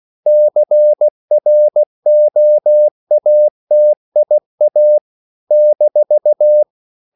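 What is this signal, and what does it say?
Morse code, heard as a keyed tone, about 16 wpm, 600 Hz. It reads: CROATIA -
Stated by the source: CW practice of national capitals at 16 words per minute, from ON6ZQ.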